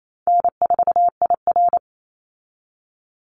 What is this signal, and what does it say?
Morse code, heard as a keyed tone, about 28 wpm, 700 Hz. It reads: N4IR
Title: Easy Call Signs - 28wpm